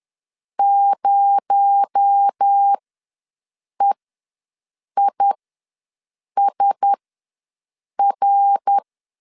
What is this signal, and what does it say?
Morse code, heard as a keyed tone, about 10 wpm, spaced as for 6 wpm, 800 Hz. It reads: 0EISR
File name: LEZIONI / 8 YJ45 / 8 YJ4 ws 10wpm 4020